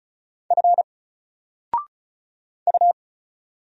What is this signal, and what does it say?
Morse code, heard as a keyed tone, about 36 wpm, 700 Hz. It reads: F U
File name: Single-Letter-Number-35wpm